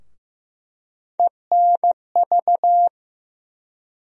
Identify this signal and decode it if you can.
Morse code, heard as a keyed tone, about 15 wpm, 700 Hz. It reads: ENV
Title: USA Call Sign Postfixes - 3 Letters 15wpm